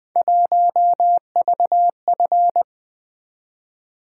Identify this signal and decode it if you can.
Morse code, heard as a keyed tone, about 20 wpm, 700 Hz. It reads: 1VF
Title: Three Letter-Number Combinations - 20wpm